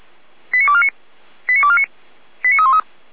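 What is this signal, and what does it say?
Five-tone selective call (ZVEI), heard as quick runs of five tones, tones of about 70 ms. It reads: 89249 89249 89313
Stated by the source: ZVEI five-tone call received by SDR, from Feuerwehr Orlamünde audio server.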